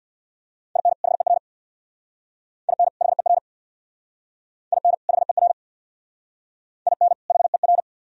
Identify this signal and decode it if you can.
Morse code, heard as a keyed tone, about 60 wpm, 700 Hz. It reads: IN HER IN HER IN HER IN HER
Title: Speed-Racing - Sets of 2 Words 50wpm